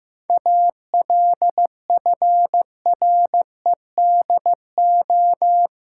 Morse code, heard as a keyed tone, about 15 wpm, 700 Hz. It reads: ALFREDO
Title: Pangram Challenge v2 - 15wpm